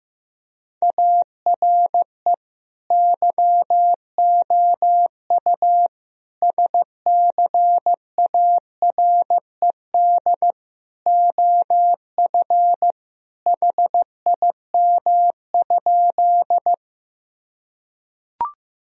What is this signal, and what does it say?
Morse code, heard as a keyed tone, about 15 wpm, 700 Hz. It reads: ARE YOU SCARED OF HIM?
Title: Sentences from Top 1000 Words - Review Part 4 - 15wpm